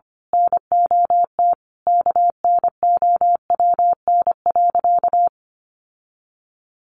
Morse code, heard as a keyed tone, about 25 wpm, 700 Hz. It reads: NOT KNOWN.
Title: Sentences from Top 1300 Words 25wpm